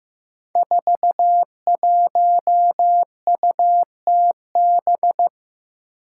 Morse code, heard as a keyed tone, about 15 wpm, 700 Hz. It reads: 41UTB
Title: Five Challenging Letter-Number Combinations 15wpm